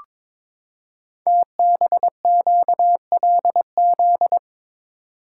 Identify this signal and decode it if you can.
Morse code, heard as a keyed tone, about 22 wpm, 700 Hz. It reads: TBQLZ
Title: Code Groups - 5 Letter:Numbers - 22wpm